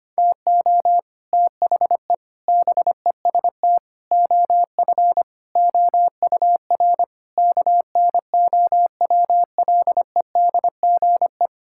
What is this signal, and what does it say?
Morse code, heard as a keyed tone, about 25 wpm, 700 Hz. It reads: TO THE BEST OF OUR KNOWLEDGE
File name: Sets of 9 Words - 25wpm